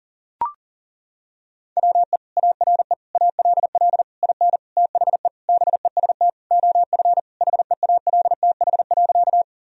Morse code, heard as a keyed tone, about 37 wpm, 700 Hz. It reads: WE ARE ALL IN THE BEST OF HEALTH.
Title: Sentences from Top 1000 Words - Review Part 3 - 40wpm